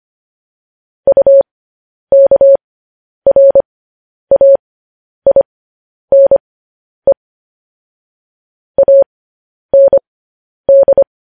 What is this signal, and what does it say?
Morse code, heard as a keyed tone, about 25 wpm, spaced as for 10 wpm, 550 Hz. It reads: UKRAINE AND
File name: podcast / News headlines-1x-25x10